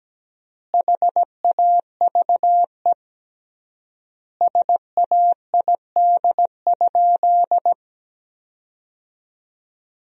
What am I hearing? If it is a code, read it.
Morse code, heard as a keyed tone, about 17 wpm, 700 Hz. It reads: HAVE SAID?
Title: 3x Word-Spacing - Sentences from Top 100 Words - 17wpm